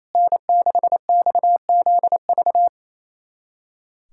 Morse code, heard as a keyed tone, about 28 wpm, 700 Hz. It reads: N6XZV
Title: USA Callsign DE Callsign - 5 Characters - 28wpm